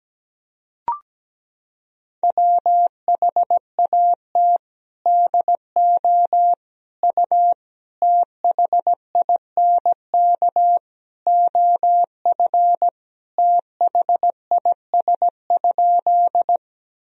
Morse code, heard as a keyed tone, about 17 wpm, 700 Hz. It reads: WHAT DO U THINK OF THIS?
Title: Sentences from Top 200 Words and Abbreviations 17wpm